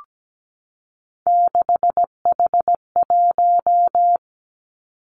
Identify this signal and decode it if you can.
Morse code, heard as a keyed tone, about 17 wpm, 700 Hz. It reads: 6H1
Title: Three Challenging Letter-Number Combinations 17wpm